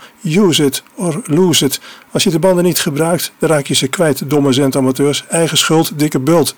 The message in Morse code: none